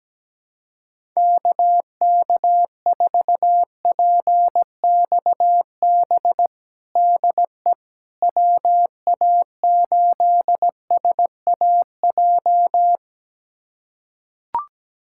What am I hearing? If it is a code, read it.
Morse code, heard as a keyed tone, about 17 wpm, 700 Hz. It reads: KK4PXB DE WA8SAJ
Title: USA Callsign DE Callsign - 6 Characters - 17wpm